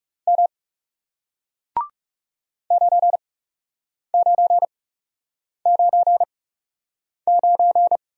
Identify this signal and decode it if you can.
Morse code, heard as a keyed tone, about 40 wpm, spaced as for 8 wpm, 700 Hz. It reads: M 9999